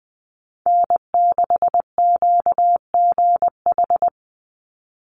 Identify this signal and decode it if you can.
Morse code, heard as a keyed tone, about 20 wpm, 700 Hz. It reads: N6QGH
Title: USA Call Signs 20wpm